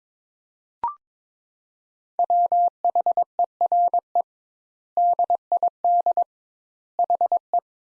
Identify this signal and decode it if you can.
Morse code, heard as a keyed tone, about 22 wpm, 700 Hz. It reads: WHERE DID HE
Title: Sentences Top 200 Words - 22wpm